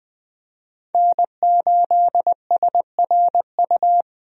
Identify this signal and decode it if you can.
Morse code, heard as a keyed tone, about 20 wpm, 700 Hz. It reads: N8SRU
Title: USA Call Signs - 5 Characters - Repeated 2x - 20wpm